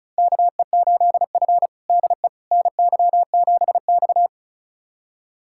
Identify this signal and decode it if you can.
Morse code, heard as a keyed tone, about 36 wpm, 700 Hz. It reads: KE8F DE NY7X
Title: USA Callsign DE Callsign - 4 Characters - 35wpm